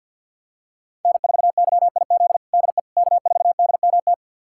Morse code, heard as a keyed tone, about 51 wpm, 700 Hz. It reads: N4YIZ DE K4DMT
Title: USA Callsign DE Callsign - 5 Characters - 50wpm